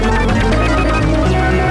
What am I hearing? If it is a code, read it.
There is no code.